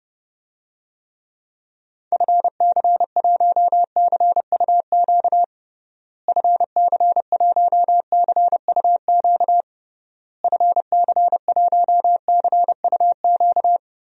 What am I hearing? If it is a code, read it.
Morse code, heard as a keyed tone, about 32 wpm, 700 Hz. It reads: FC1CUQ FC1CUQ FC1CUQ
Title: DXCC-100 Call Signs - 6 Characters - Repeated 3x - 30wpm